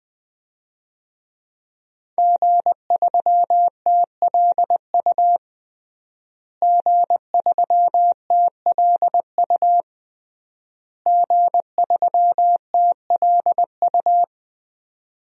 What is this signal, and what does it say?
Morse code, heard as a keyed tone, about 20 wpm, 700 Hz. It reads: G3TLU G3TLU G3TLU
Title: DXCC-100 Call Signs - 5 Characters - Repeated 3x - 20wpm